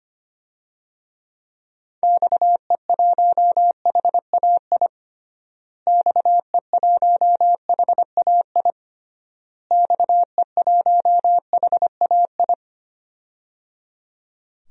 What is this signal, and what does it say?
Morse code, heard as a keyed tone, about 25 wpm, 700 Hz. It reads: XE1HAI XE1HAI XE1HAI
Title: DXCC-100 Call Signs - 6 Characters - Repeated 3x - 25wpm